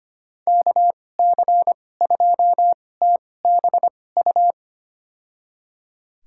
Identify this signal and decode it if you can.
Morse code, heard as a keyed tone, about 25 wpm, spaced as for 18 wpm, 700 Hz. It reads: KC2TBU
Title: USA Call Signs - 6 Characters - 2x Character Spacing - 25wpm